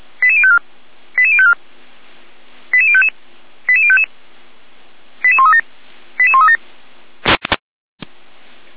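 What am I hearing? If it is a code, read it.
Five-tone selective call (ZVEI), heard as quick runs of five tones, tones of about 70 ms. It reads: 88054 88054 88055 88055 88137 88137
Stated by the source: Received on an SDR; ZVEI five-tone call 88190.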